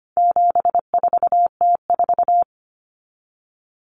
Morse code, heard as a keyed tone, about 25 wpm, 700 Hz. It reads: 74T4
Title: Common Cut Numbers - 4-Digit 25wpm